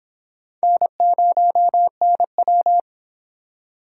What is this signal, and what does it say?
Morse code, heard as a keyed tone, about 25 wpm, 700 Hz. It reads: N0NW